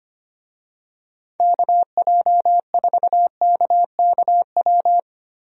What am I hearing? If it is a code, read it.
Morse code, heard as a keyed tone, about 25 wpm, 700 Hz. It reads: KJ4KKW
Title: USA Call Signs - 6 Characters - 25wpm